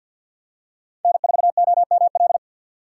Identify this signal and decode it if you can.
Morse code, heard as a keyed tone, about 48 wpm, 700 Hz. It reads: N4OML
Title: USA Call Signs - 3x Character Spacing - 50wpm